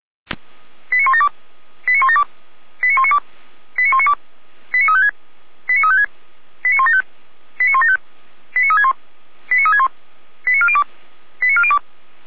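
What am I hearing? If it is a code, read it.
Five-tone selective call (ZVEI), heard as quick runs of five tones, tones of about 70 ms. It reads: 89172 89172 89192 89192 89356 89356 89175 89175 89361 89361 89402 89402
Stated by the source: ZVEI five-tone call received by SDR.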